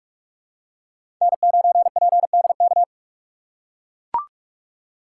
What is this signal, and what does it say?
Morse code, heard as a keyed tone, about 44 wpm, 700 Hz. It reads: N9PDK
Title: USA Call Signs - 6x Character Spacing - 45wpm